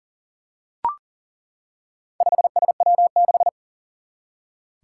Morse code, heard as a keyed tone, about 40 wpm, 700 Hz. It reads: 5SW6
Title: Four Challenging Letter-Number Combinations 40wpm